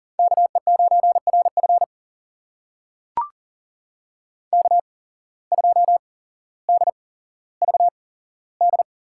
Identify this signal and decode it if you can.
Morse code, heard as a keyed tone, about 40 wpm, spaced as for 17 wpm, 700 Hz. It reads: KE9RF K2DVD
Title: USA Call Signs - 5 Characters - 8x Character Spacing - 40wpm